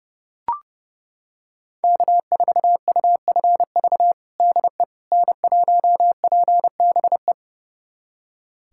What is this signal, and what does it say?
Morse code, heard as a keyed tone, about 32 wpm, 700 Hz. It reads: K4UFV DE N1PBE
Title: USA Callsign DE Callsign - 5 Characters - 30wpm